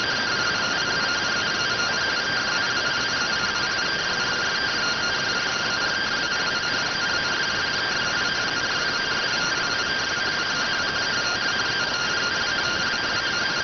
Two-tone teleprinter signal, about 170 Hz shift.